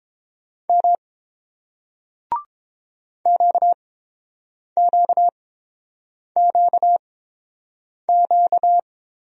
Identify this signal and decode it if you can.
Morse code, heard as a keyed tone, about 30 wpm, spaced as for 7 wpm, 700 Hz. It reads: M QQQQ